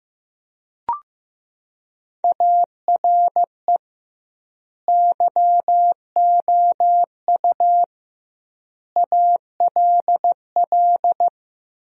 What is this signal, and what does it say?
Morse code, heard as a keyed tone, about 15 wpm, 700 Hz. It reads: ARE YOU ALL
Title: 2x Word-Spacing - Sentences from Top 400 Words - 15wpm